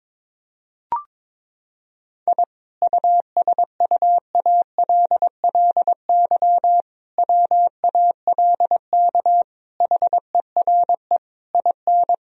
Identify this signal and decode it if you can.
Morse code, heard as a keyed tone, about 22 wpm, 700 Hz. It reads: I USUALLY WALK HERE IN
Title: Sentences from Top 1000 Words - Review Part 3 - 22wpm